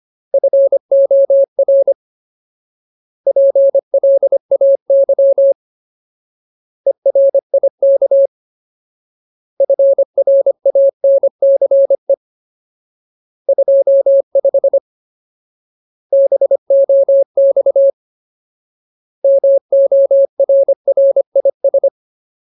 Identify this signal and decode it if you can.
Morse code, heard as a keyed tone, about 25 wpm, 550 Hz. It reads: FOR PLAY ERIK FRANCE 25 BOX MORRIS
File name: Ultimate Word List / 25WPM Ultimate